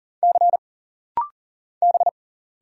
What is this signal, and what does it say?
Morse code, heard as a keyed tone, about 39 wpm, 700 Hz. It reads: C B
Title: Single Letter-Number Alt - Rapid-Fire 40wpm